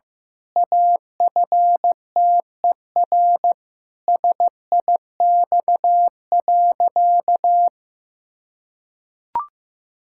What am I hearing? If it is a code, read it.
Morse code, heard as a keyed tone, about 16 wpm, 700 Hz. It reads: AFTER SIX.
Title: Pangram Challenge v2 - 15wpm